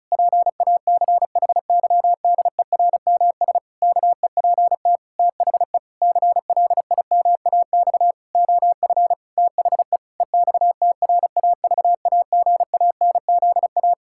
Morse code, heard as a keyed tone, about 36 wpm, 700 Hz. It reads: PACHYDERMS KEPT THE CLIMAX OF THE EXTRAVAGANZA
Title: Pangram Challenge v2 - 35wpm